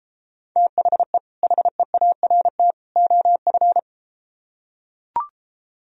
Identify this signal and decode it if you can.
Morse code, heard as a keyed tone, about 33 wpm, 700 Hz. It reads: THE HEART OF